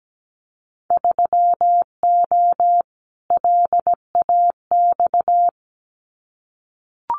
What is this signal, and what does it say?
Morse code, heard as a keyed tone, about 17 wpm, 700 Hz. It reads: 3O LAX
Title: Winter Field Day Exchanges Encore 17wpm